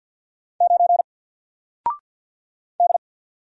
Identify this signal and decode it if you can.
Morse code, heard as a keyed tone, about 48 wpm, 700 Hz. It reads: 9 D
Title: Single Letter-Number 50wpm